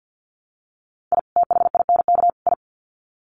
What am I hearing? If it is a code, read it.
Morse code, heard as a keyed tone, about 50 wpm, 700 Hz. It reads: I THINK I
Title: Sets of 3 Words 50wpm